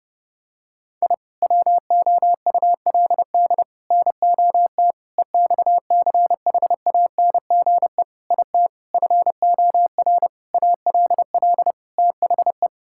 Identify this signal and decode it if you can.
Morse code, heard as a keyed tone, about 30 wpm, 700 Hz. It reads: I WOULD NOT EXCHANGE IT FOR ALL THE
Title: Sentences from Top 1000 Words - Review Part 6 - 30wpm